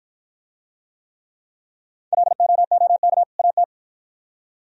Morse code, heard as a keyed tone, about 53 wpm, 690 Hz. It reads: LOOK AT